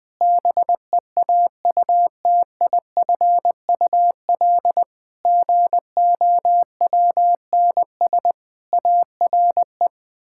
Morse code, heard as a keyed tone, about 20 wpm, 700 Hz. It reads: BEAUTIFUL GOWNS ARE
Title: Pangram Challenge v2 - 20wpm